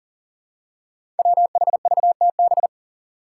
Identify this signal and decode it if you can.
Morse code, heard as a keyed tone, about 40 wpm, 700 Hz. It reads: WHVTB
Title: Five Challenging Letter-Number Combinations 40wpm